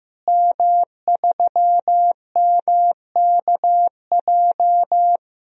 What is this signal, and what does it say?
Morse code, heard as a keyed tone, about 15 wpm, 700 Hz. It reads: M3MKJ
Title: DXCC-100 Call Signs - 5 Characters - Repeated 3x - 15wpm